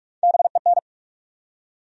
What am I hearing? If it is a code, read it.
Morse code, heard as a keyed tone, about 46 wpm, 700 Hz. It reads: BEN
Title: USA Common Names - Max 5 Letters - 45wpm